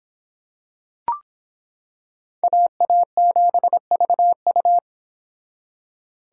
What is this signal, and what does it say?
Morse code, heard as a keyed tone, about 25 wpm, 700 Hz. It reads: AA7VU